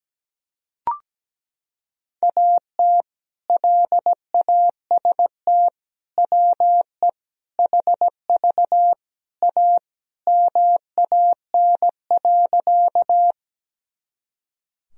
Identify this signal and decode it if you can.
Morse code, heard as a keyed tone, about 17 wpm, 700 Hz. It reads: AT LAST WE HV A MAN.
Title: Sentences from Top 300 Words and Abbreviations 17wpm